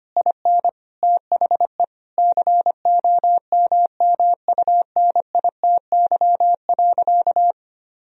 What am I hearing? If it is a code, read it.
Morse code, heard as a keyed tone, about 25 wpm, 700 Hz. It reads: IN THE COMMUNITY.
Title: Sentences from Top 1000 Words - Review Part 4 - 25wpm